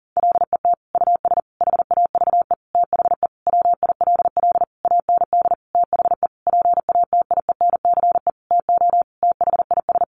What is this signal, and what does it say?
Morse code, heard as a keyed tone, about 40 wpm, 700 Hz. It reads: LET US HAVE THE WILL AND THE PATIENCE TO THIS
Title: Sentences from Top 1300 Words 40wpm